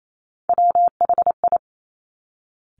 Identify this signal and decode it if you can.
Morse code, heard as a keyed tone, about 28 wpm, 700 Hz. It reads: WHI
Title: Top Three Letter Combinations in Words 28wpm